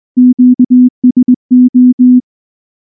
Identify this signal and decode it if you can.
Morse code, heard as a keyed tone, about 19 wpm, 250 Hz. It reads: QSO
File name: morse / morse-training-20WPM-9